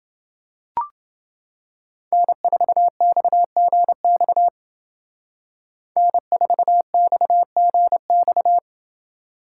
Morse code, heard as a keyed tone, about 29 wpm, 700 Hz. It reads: N4XGX N4XGX